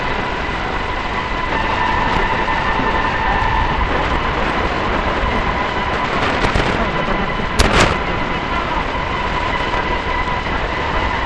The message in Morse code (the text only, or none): none